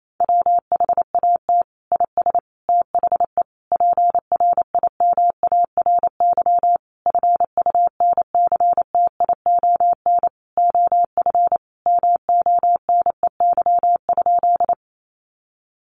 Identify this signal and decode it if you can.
Morse code, heard as a keyed tone, about 28 wpm, 700 Hz. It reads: WHAT IS THE PRIMARY FUNCTION OF MONEY?
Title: Sentences from Top 1100 Words 28wpm